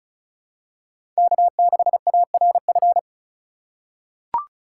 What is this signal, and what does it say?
Morse code, heard as a keyed tone, about 36 wpm, 700 Hz. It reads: K6ARF
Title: USA Call Signs - 5 Characters - 3x Character Spacing - 35wpm